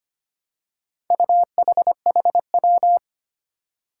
Morse code, heard as a keyed tone, about 25 wpm, 700 Hz. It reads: UHHW
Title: Four Challenging Letter-Number Combinations 25wpm